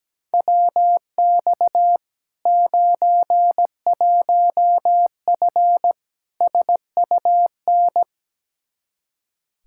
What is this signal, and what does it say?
Morse code, heard as a keyed tone, about 17 wpm, 700 Hz. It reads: WX 91F SUN
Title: Practice QSO - Exchange 2 - Info - 17wpm